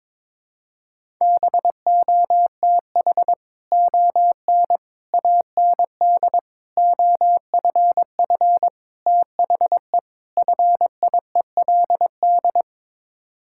Morse code, heard as a keyed tone, about 22 wpm, 700 Hz. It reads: BOTH ON AND OFF THE FIELD
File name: practice / 22wpm / Sets of 6 Words - 22wpm